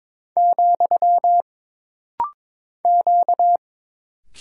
Morse code, heard as a keyed tone, about 22 wpm, 700 Hz. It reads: , Q
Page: This rapid-fire timing is geared to push your ICR (Instant Character Recognition) skills.